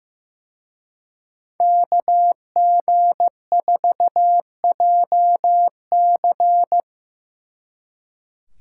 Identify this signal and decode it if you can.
Morse code, heard as a keyed tone, about 18 wpm, 700 Hz. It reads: KG4JC
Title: Speed Racing - USA 2x2 Call Signs 15wpm